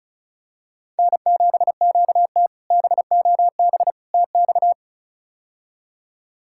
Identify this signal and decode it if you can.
Morse code, heard as a keyed tone, about 36 wpm, 700 Hz. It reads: N7QT BOB TX
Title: K1USN SST QSO Practice - 35wpm